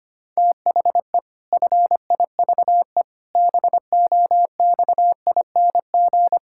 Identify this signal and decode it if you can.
Morse code, heard as a keyed tone, about 25 wpm, 700 Hz. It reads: THE FIVE BOXING
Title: Pangram Challenge v2 - 25wpm